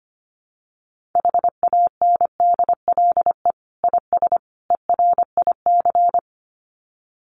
Morse code, heard as a keyed tone, about 25 wpm, 700 Hz. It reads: HANDLE IS ERIC
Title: Common QSO Phrases 25wpm